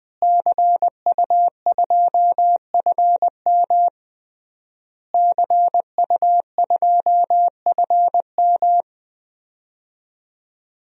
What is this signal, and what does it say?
Morse code, heard as a keyed tone, about 21 wpm, 700 Hz. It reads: CU2FM CU2FM
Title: DXCC-100 Call Signs - 5 Characters - Repeated 3x - 20wpm